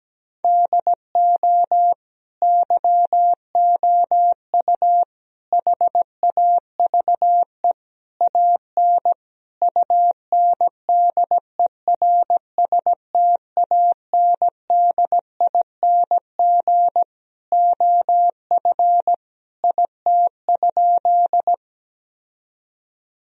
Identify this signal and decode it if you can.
Morse code, heard as a keyed tone, about 17 wpm, 700 Hz. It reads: DO YOU HAVE AN UNDERSTANDING OF IT?
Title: Sentences from Top 1000 Words - Review Part 3 - 17wpm